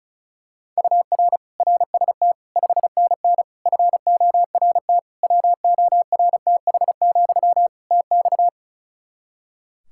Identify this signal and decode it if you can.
Morse code, heard as a keyed tone, about 34 wpm, 700 Hz. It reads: UR RST 5NN FORT WORTH, TX
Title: Practice QSO - Exchange 1 - Beginning - 35wpm